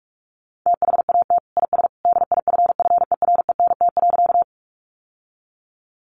Morse code, heard as a keyed tone, about 44 wpm, 700 Hz. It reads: THAT IS DIFFERENT.
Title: Sentences from Top 600 Words 45wpm